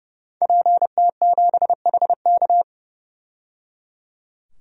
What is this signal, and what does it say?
Morse code, heard as a keyed tone, about 30 wpm, 700 Hz. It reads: PT7HK
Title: DXCC-100 Call Signs - 5 Characters - 30wpm